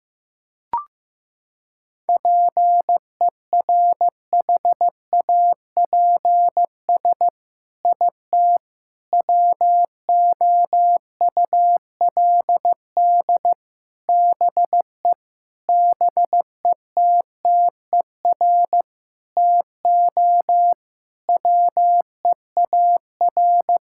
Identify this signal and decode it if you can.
Morse code, heard as a keyed tone, about 15 wpm, 700 Hz. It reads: PERHAPS IT WOULD BE BETTER TO WEAR
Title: Sentences from Top 1600 Words 15wpm